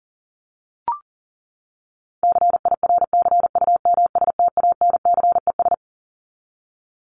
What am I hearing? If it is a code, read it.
Morse code, heard as a keyed tone, about 40 wpm, 700 Hz. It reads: CIRCUMSTANCES
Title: ES:S Suffix - Top 100 Words with Base Words 40wpm